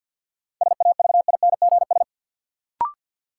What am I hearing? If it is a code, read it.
Morse code, heard as a keyed tone, about 52 wpm, 700 Hz. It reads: SAVINGS
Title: Sentences from Top 1200 Words - New Word Drill - 50wpm